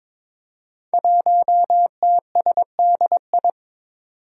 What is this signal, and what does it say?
Morse code, heard as a keyed tone, about 22 wpm, 700 Hz. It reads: 1TSDI
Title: Five Challenging Letter-Number Combinations 22wpm